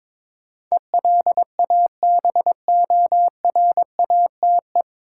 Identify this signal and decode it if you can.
Morse code, heard as a keyed tone, about 23 wpm, 700 Hz. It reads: ELABORATE
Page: Morse Code Ninja practice 22wpm